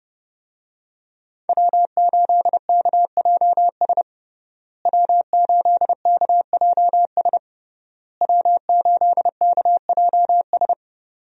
Morse code, heard as a keyed tone, about 29 wpm, 700 Hz. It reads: W8KJS W8KJS W8KJS
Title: USA Call Signs - 5 Characters - Repeated 3x - 30wpm